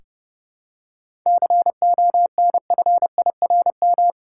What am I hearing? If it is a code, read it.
Morse code, heard as a keyed tone, about 29 wpm, 700 Hz. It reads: CONFIRM
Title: Seven Letter Words - 30wpm